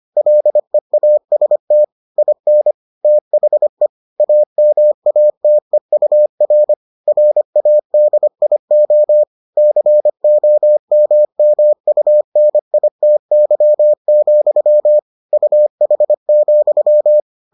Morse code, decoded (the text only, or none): LEAST IN THE AMATEUR RADIO COMMUNITY, UH,